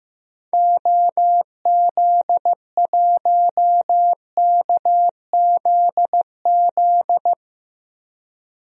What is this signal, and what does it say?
Morse code, heard as a keyed tone, about 15 wpm, 700 Hz. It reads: OZ1KZZ
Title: DXCC-100 Call Signs - 6 Characters - Repeated 2x - 15wpm